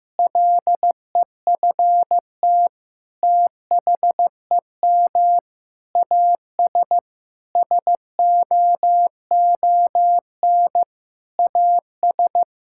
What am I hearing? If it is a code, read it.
Morse code, heard as a keyed tone, about 15 wpm, 700 Hz. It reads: LEFT THEM AS SOON AS
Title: Sentences from Top 1000 Words - Review Part 5 - 15wpm